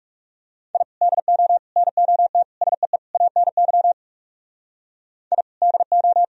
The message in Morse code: I DO NOT SEE ANY I DO

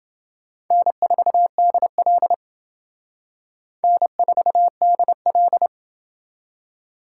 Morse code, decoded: N4DL N4DL

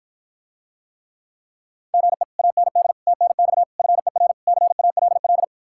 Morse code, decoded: GE AND TNX FER CALL

